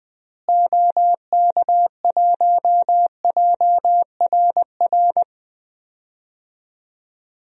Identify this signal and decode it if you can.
Morse code, heard as a keyed tone, about 21 wpm, 700 Hz. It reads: OK1JRR
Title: DXCC-100 Call Signs - 6 Characters - Repeated 3x - 20wpm